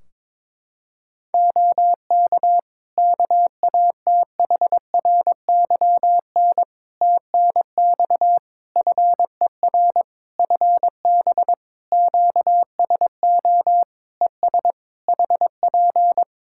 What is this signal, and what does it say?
Morse code, heard as a keyed tone, about 22 wpm, 700 Hz. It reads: OK KATHRYN TNX FER FB QSO ES HP
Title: K1USN SST QSO Practice 22wpm